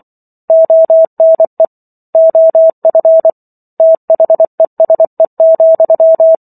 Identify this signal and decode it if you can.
Morse code, heard as a keyed tone, about 24 wpm, 650 Hz. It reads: ONE OF THESE,